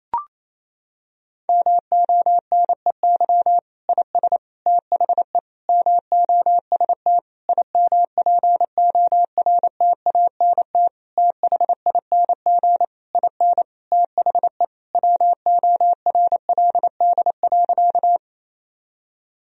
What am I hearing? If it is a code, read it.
Morse code, heard as a keyed tone, about 28 wpm, 700 Hz. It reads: MONEY IS THE MOST IMPORTANT THING IN THE WORLD.
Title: Sentences from Top 1000 Words - Review Part 1 - 28wpm